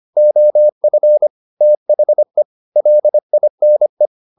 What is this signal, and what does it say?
Morse code, heard as a keyed tone, about 25 wpm, 600 Hz. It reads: OF THE LINE
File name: the old man and the sea part 1 25wpm